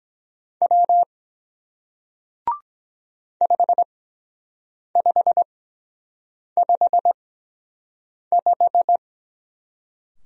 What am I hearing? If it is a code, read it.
Morse code, heard as a keyed tone, about 23 wpm, spaced as for 7 wpm, 700 Hz. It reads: W 5555